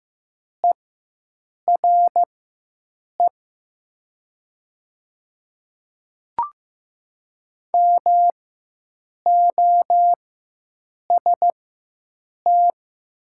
Morse code, heard as a keyed tone, about 15 wpm, spaced as for 7 wpm, 700 Hz. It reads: ERE MOST
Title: 4x Character Spacing - Top 100 Words - 15wpm